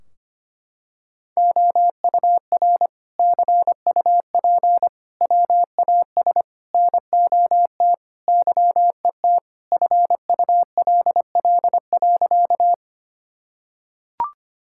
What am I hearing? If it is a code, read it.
Morse code, heard as a keyed tone, about 25 wpm, 700 Hz. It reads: OUR CUP WAS NOT YET FULL.
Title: Sentences from Top 1600 Words 25wpm